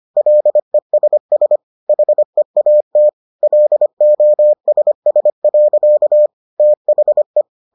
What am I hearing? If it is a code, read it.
Morse code, heard as a keyed tone, about 25 wpm, 600 Hz. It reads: LESS HEAT LOSS. THE